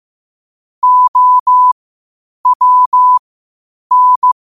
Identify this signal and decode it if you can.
Morse code, heard as a keyed tone, about 15 wpm, spaced as for 8 wpm, 1000 Hz. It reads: OWN
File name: 8wpm